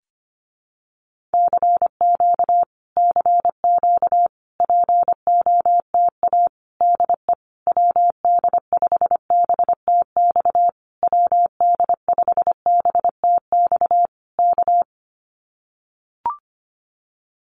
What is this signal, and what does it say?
Morse code, heard as a keyed tone, about 25 wpm, 700 Hz. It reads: CQ CQ POTA DE WD5BTX WD5BTX K
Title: Common QSO Phrases Encore 25wpm